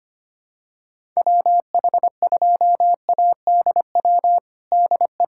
Morse code, heard as a keyed tone, about 25 wpm, 700 Hz. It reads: WH2ADW DE
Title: USA Callsign DE Callsign - 6 Characters - 25wpm